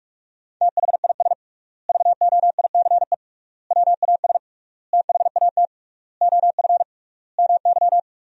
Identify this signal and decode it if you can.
Morse code, heard as a keyed tone, about 41 wpm, 700 Hz. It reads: THIS VOICE WAS THAT OF MY
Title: 3x Word-Spacing - Sentences from Top 500 Words - 45wpm